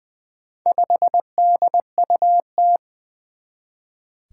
Morse code, heard as a keyed tone, about 21 wpm, 700 Hz. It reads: 5DUT
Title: Four Challenging Letter-Number Combinations 20wpm